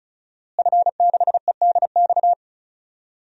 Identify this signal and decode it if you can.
Morse code, heard as a keyed tone, about 36 wpm, 700 Hz. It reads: F6EDX
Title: DXCC-100 Call Signs - 5 Characters - Repeated 3x - 35wpm